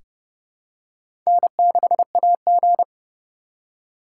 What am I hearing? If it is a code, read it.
Morse code, heard as a keyed tone, about 30 wpm, 700 Hz. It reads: N6AG
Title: USA Call Signs - 4 Characters - 3x Character Spacing - 30wpm